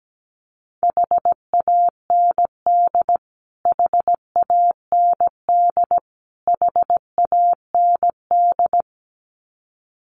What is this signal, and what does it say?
Morse code, heard as a keyed tone, about 17 wpm, 700 Hz. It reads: HAND HAND HAND
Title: Top 500 Words Repeated 17wpm